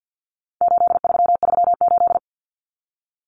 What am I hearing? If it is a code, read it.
Morse code, heard as a keyed tone, about 50 wpm, 700 Hz. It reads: 8338
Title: Numbers - 4-Digit 50wpm